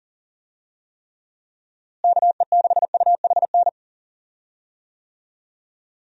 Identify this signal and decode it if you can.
Morse code, heard as a keyed tone, about 40 wpm, 700 Hz. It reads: KE6UHN
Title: USA Call Signs - 6 Characters - Repeated 3x - 40wpm